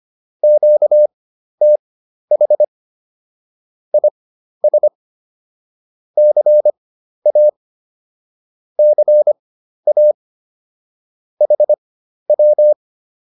Morse code, heard as a keyed tone, about 25 wpm, spaced as for 12 wpm, 600 Hz. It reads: QTH IS CA CA HW